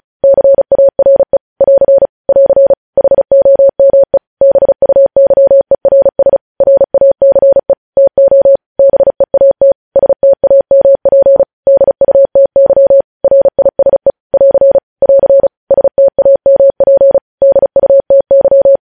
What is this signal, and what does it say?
Morse code, heard as a keyed tone, about 36 wpm, 550 Hz. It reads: CARE <AR> <AR> HOME BUYERS RACE TO BEAT STAMP DUTY RISE <AR> <AR> STAMP DUTY
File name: podcast / News headlines-1x-35x35